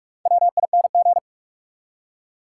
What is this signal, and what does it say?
Morse code, heard as a keyed tone, about 43 wpm, 700 Hz. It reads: WING
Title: Four Letter Words - 45wpm